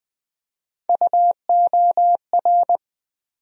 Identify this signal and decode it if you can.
Morse code, heard as a keyed tone, about 20 wpm, 700 Hz. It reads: UOR
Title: USA Call Sign Postfixes - 3 Letters 20wpm